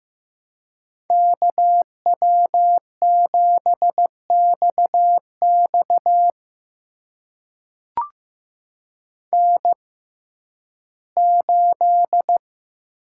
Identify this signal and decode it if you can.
Morse code, heard as a keyed tone, about 15 wpm, spaced as for 11 wpm, 700 Hz. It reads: KW7XX N8
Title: USA Call Signs - 5 Characters - 6x Character Spacing - 15wpm